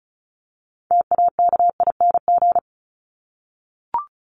Morse code, heard as a keyed tone, about 34 wpm, 700 Hz. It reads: TAKING
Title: ING Suffix - Top 100 Words with Base Words 35wpm